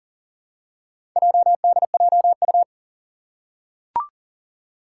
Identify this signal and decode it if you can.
Morse code, heard as a keyed tone, about 40 wpm, 700 Hz. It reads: JDJU
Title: Four Challenging Letter-Number Combinations 40wpm